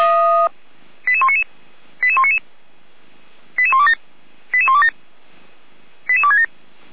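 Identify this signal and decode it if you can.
Five-tone selective call (ZVEI), heard as quick runs of five tones, tones of about 70 ms. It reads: 88199 88199 88127 88127 88267
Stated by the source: ZVEI five-tone call 88129 received by SDR, archived by fire brigade Feuerwehr Orlamünde.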